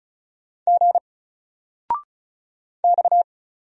Morse code, heard as a keyed tone, about 36 wpm, 700 Hz. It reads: G X
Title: Single Letter-Number 35wpm